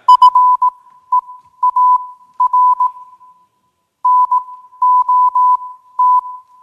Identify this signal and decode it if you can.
Morse code, heard as a keyed tone, about 18 wpm, spaced as for 12 wpm, 1000 Hz. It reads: FEAR NOT